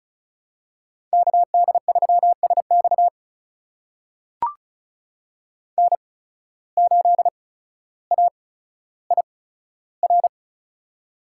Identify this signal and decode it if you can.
Morse code, heard as a keyed tone, about 36 wpm, spaced as for 15 wpm, 700 Hz. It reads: KD3SX N8AIR
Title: USA Call Signs - 5 Characters - 8x Character Spacing - 35wpm